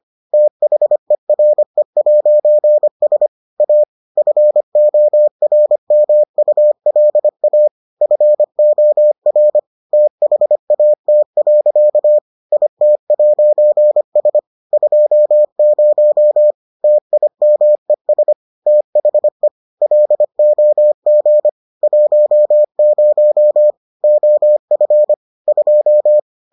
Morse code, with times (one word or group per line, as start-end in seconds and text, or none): 0.3-3.3 THERE'S
3.6-3.8 A
4.2-7.7 FORMULA
8.0-9.6 FOR
9.9-12.2 THAT.
12.5-14.4 IT'S
14.7-16.5 20
16.8-18.3 TIMES
18.7-19.5 THE
19.8-21.5 LOG
21.8-23.7 10
24.0-25.1 OF
25.5-26.2 2